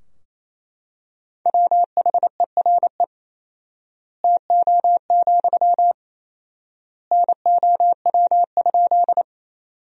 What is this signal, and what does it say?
Morse code, heard as a keyed tone, about 28 wpm, 700 Hz. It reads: WHERE TO, NOW?
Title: 4x Word-Spacing - Sentences from Top 200 Words - 28wpm